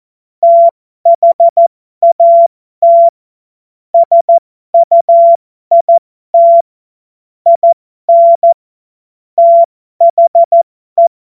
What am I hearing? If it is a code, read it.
Morse code, heard as a keyed tone, about 14 wpm, spaced as for 12 wpm, 680 Hz. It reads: THAT SUIT IN THE
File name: Morsekurs / Witze